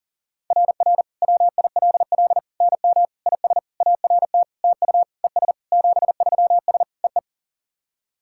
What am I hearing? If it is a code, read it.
Morse code, heard as a keyed tone, about 40 wpm, 690 Hz. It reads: RR WILL NM IS ART TU ES 73S EE
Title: SOTA QSO Practice 40wpm